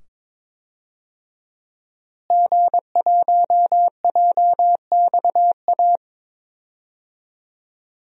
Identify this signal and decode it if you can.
Morse code, heard as a keyed tone, about 22 wpm, 700 Hz. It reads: G1JXA
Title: DXCC-100 Call Signs - 5 Characters - Repeated 2x - 22wpm